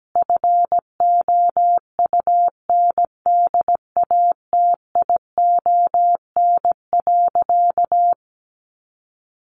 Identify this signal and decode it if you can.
Morse code, heard as a keyed tone, about 17 wpm, 700 Hz. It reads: FOUNDATION.
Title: Sentences from Top 1200 Words 17wpm